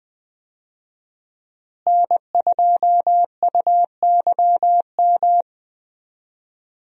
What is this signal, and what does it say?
Morse code, heard as a keyed tone, about 20 wpm, 700 Hz. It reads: N2UYM
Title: USA Call Signs - 5 Characters - Repeated 2x - 20wpm